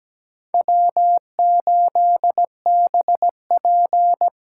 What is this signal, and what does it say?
Morse code, heard as a keyed tone, about 20 wpm, 700 Hz. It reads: W8BP
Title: Speed Racing - USA 1x2 Call Signs 17wpm